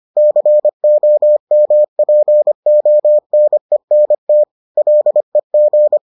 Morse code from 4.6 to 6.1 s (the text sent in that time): LEG